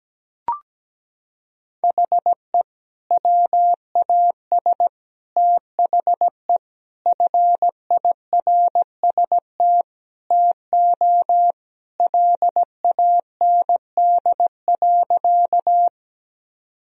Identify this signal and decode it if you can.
Morse code, heard as a keyed tone, about 17 wpm, 700 Hz. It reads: HE WAS THE FIRST TO LAND.
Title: Sentences from Top 1000 Words - Review Part 1 - 17wpm